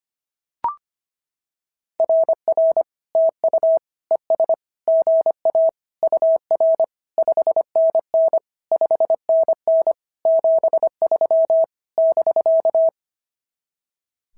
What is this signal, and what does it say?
Morse code, heard as a keyed tone, about 24 wpm, 650 Hz. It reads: RR TU ES GA UR 5NN 5NN 73 <BK>
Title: SOTA QSO Practice 25wpm